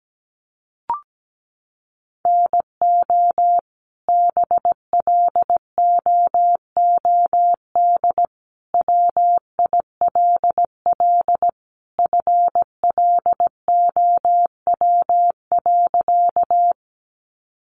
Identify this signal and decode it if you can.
Morse code, heard as a keyed tone, about 17 wpm, 700 Hz. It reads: NO BLOOD WILL FLOW.
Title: Sentences from Top 1200 Words 17wpm